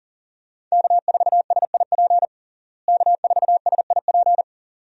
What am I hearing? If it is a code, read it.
Morse code, heard as a keyed tone, about 40 wpm, 700 Hz. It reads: K4SIP K4SIP
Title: USA Call Signs - 5 Characters - Repeated 2x - 40wpm